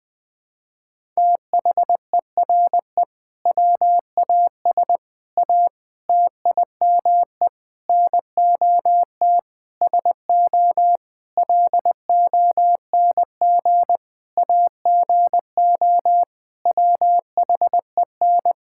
Morse code, heard as a keyed tone, about 20 wpm, 700 Hz. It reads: THERE WAS A TIME NOT SO LONG AGO WHEN
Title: Sentences with 8-Word Phrases - Spoken Beginning - 20wpm